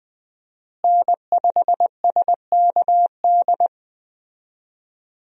Morse code, heard as a keyed tone, about 20 wpm, 700 Hz. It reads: N5SKD
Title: USA Call Signs - 5 Characters - Repeated 2x - 20wpm